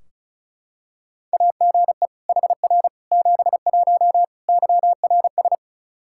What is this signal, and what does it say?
Morse code, heard as a keyed tone, about 36 wpm, 700 Hz. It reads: AGE HR 71 YRS
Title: Practice QSO - Exchange 3 - More Info - 35wpm